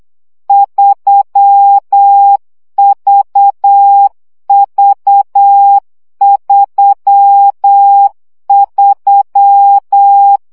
Morse code, heard as a keyed tone, about 8 wpm, 800 Hz. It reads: 3VV33
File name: CW gyakorlas v 3 7 baud